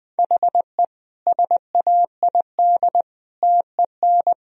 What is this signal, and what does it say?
Morse code, heard as a keyed tone, about 21 wpm, 700 Hz. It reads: HE SAID TEN
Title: Sentences from Top 1000 Words - Review Part 3 - 20wpm